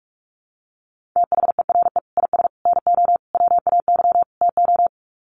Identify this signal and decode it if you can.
Morse code, heard as a keyed tone, about 44 wpm, 700 Hz. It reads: THERE IS NO WAY TO
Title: Sets of 5 Words Encore 45wpm